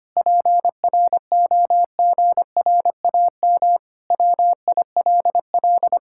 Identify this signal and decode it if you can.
Morse code, heard as a keyed tone, about 25 wpm, 700 Hz. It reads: PROGRAM WILL
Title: Sentences from Top 1000 Words - Review Part 5 - 25wpm